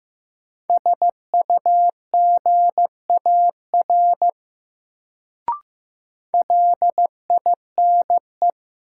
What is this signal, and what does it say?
Morse code, heard as a keyed tone, about 15 wpm, 700 Hz. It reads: SUGAR LINE
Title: Sentences from Top 1200 Words - New Word Drill Alt - 15wpm